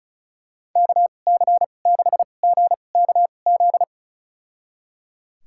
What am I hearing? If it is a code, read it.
Morse code, heard as a keyed tone, about 36 wpm, spaced as for 25 wpm, 700 Hz. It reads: KC6GKZ
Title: USA Call Signs - 2x Character Spacing - 35wpm